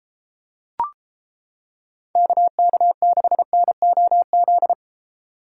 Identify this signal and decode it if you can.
Morse code, heard as a keyed tone, about 31 wpm, 700 Hz. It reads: KK6NOZ